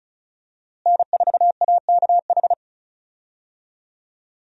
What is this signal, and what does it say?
Morse code, heard as a keyed tone, about 36 wpm, 700 Hz. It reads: N4AKH
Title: USA Call Signs - 5 Characters - Repeated 3x - 35wpm